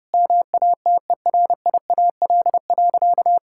Morse code, heard as a keyed tone, about 30 wpm, 700 Hz. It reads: MATERIAL.
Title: Sentences from Top 1000 Words - Review Part 5 - 30wpm